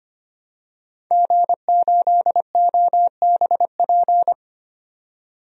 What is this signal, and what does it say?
Morse code, heard as a keyed tone, about 25 wpm, 700 Hz. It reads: G8OBP
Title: DXCC-100 Call Signs - 5 Characters - 25wpm